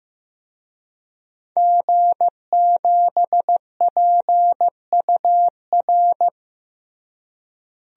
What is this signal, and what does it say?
Morse code, heard as a keyed tone, about 15 wpm, 700 Hz. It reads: G7PUR
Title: DXCC-100 Call Signs - 5 Characters - Repeated 3x - 15wpm